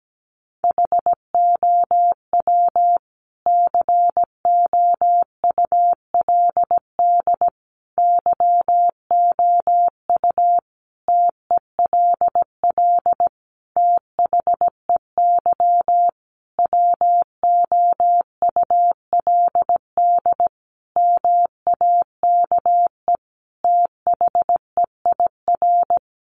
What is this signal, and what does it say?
Morse code, heard as a keyed tone, about 17 wpm, 700 Hz. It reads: HOW COULD YOU TELL THEY WOULD MAKE THEIR
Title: Sentences from Top 1000 Words 17wpm